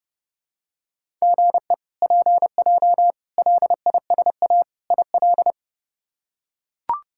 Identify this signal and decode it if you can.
Morse code, heard as a keyed tone, about 30 wpm, 700 Hz. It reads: GE PJ LISA IL
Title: K1USN SST QSO Practice - 30wpm